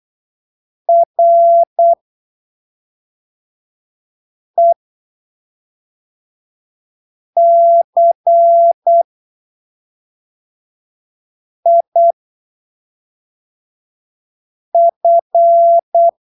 Morse code, heard as a keyed tone, about 8 wpm, spaced as for 3 wpm, 680 Hz. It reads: RECIF
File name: arquivos / texto / texto 16 08ppm